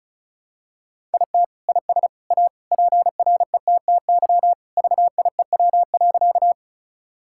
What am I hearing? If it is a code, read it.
Morse code, heard as a keyed tone, about 36 wpm, 700 Hz. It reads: IT IS A PRETTY VIEW.
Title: Sentences from Top 1000 Words - Review Part 6 - 35wpm